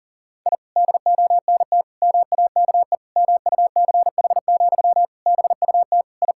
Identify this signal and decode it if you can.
Morse code, heard as a keyed tone, about 40 wpm, 700 Hz. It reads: I DONT MAKE MUCH, BUT I
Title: Sentences from Top 1000 Words - Review Part 5 - 40wpm